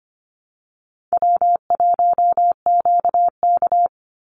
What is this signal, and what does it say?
Morse code, heard as a keyed tone, about 25 wpm, 700 Hz. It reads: W1QK
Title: Easy Call Signs - 25wpm